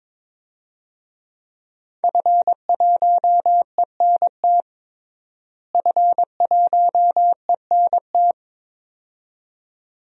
Morse code, heard as a keyed tone, about 22 wpm, 700 Hz. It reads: F1ENT F1ENT
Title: DXCC-100 Call Signs - 5 Characters - Repeated 2x - 22wpm